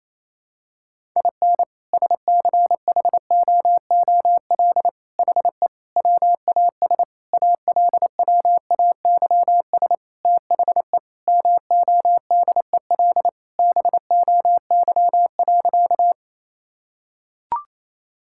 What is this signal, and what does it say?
Morse code, heard as a keyed tone, about 28 wpm, 700 Hz. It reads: IN SCHOOL HE WAS ALWAYS THE MODEL BOY.
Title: Sentences from Top 1000 Words - Review Part 5 - 28wpm